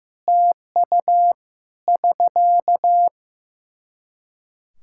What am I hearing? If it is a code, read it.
Morse code, heard as a keyed tone, about 16 wpm, 700 Hz. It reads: TU <SK>
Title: Practice QSO - Exchange 4 - Ending - 15wpm